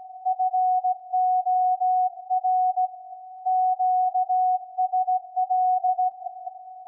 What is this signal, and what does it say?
Morse code, heard as a keyed tone, about 16 wpm, 740 Hz. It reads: FOR QSL
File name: iz7slz / VLF / SAQ 24-12-2021